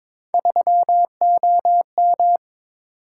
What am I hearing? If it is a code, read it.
Morse code, heard as a keyed tone, about 22 wpm, 700 Hz. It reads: 3OM
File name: practice / 22wpm / Three Letter-Number Combinations - 22wpm